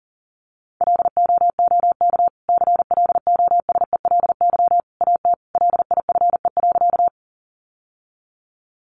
Morse code, heard as a keyed tone, about 40 wpm, 700 Hz. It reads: LOOK CLOSELY AT LIFE.
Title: Sentences from Top 1400 Words 40wpm